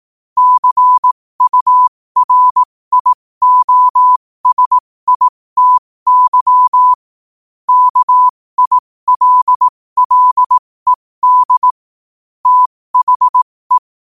Morse code, heard as a keyed tone, about 18 wpm, spaced as for 15 wpm, 1000 Hz. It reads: CURIOSITY KILLED THE
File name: qotdfiles / 15wpm